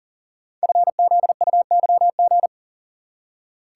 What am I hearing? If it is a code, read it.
Morse code, heard as a keyed tone, about 42 wpm, 690 Hz. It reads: FZUYG
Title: Code Groups - 5 Letter:Numbers - 40wpm